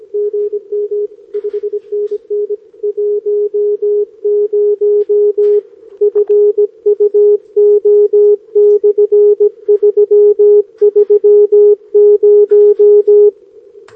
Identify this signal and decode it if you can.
Morse code, heard as a keyed tone, about 21 wpm, 410 Hz. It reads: GM 5NN 10 FUO/330
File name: OK QRP 2026 / OKQRP2026